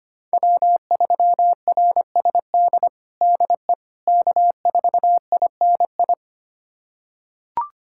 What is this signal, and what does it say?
Morse code, heard as a keyed tone, about 25 wpm, 700 Hz. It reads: W3RSD DE K4INI
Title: USA Callsign DE Callsign - 5 Characters - 25wpm